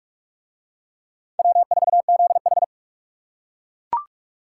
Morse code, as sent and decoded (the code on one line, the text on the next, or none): .-- ....- --.. ....
W4ZH